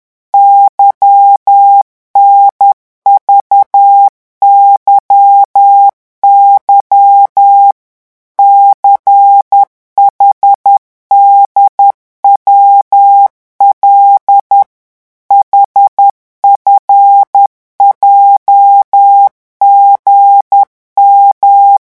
Morse code, heard as a keyed tone, about 10 wpm, 800 Hz. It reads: YNVYY CHDWL HFJGM